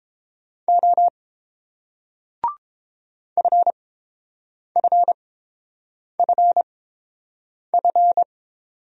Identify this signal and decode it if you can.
Morse code, heard as a keyed tone, about 29 wpm, spaced as for 7 wpm, 690 Hz. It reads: O FFFF